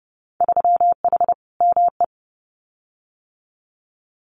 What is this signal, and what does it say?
Morse code, heard as a keyed tone, about 30 wpm, 700 Hz. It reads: 3H ME
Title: Winter Field Day Exchanges Encore 30wpm